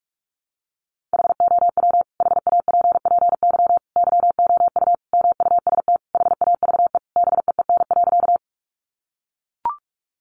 Morse code, heard as a keyed tone, about 44 wpm, 700 Hz. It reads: HOW HAPPY YOU MUST HAVE BEEN.
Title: Sentences from Top 1000 Words 45wpm